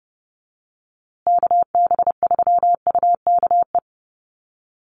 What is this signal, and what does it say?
Morse code, heard as a keyed tone, about 30 wpm, 700 Hz. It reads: KB3UKE
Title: USA Call Signs Repeated 2x 30wpm